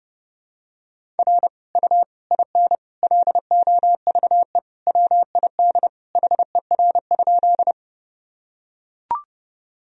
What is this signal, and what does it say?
Morse code, heard as a keyed tone, about 32 wpm, 700 Hz. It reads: R U IN LOVE WID HER?
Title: Sentences from Top 400 Words and Abbreviations 30wpm